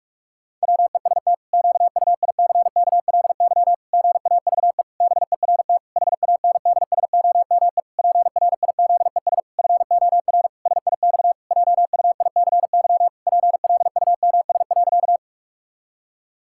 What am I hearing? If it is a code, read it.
Morse code, heard as a keyed tone, about 46 wpm, 700 Hz. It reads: WEST QUICKLY GAVE BERT HANDSOME PRIZES FOR SIX JUICY PLUMS.
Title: Pangram Challenge v2 - 45wpm